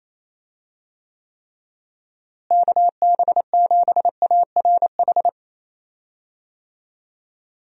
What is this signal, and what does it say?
Morse code, heard as a keyed tone, about 28 wpm, 700 Hz. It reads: KB7ARH
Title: USA Call Signs - 6 Characters - Repeated 3x - 28wpm